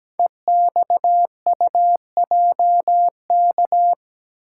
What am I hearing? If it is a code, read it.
Morse code, heard as a keyed tone, about 17 wpm, 700 Hz. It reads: EXUJK
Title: Code Groups - 5 Letters - 17wpm